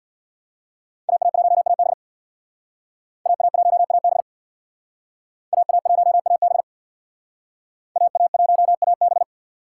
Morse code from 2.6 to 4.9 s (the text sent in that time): AA1AB